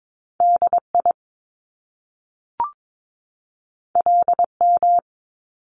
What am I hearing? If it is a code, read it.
Morse code, heard as a keyed tone, about 22 wpm, 700 Hz. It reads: DI LM
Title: Top Two Letter Combinations in Words 22wpm